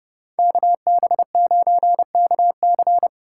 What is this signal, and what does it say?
Morse code, heard as a keyed tone, about 27 wpm, 700 Hz. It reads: KB9KC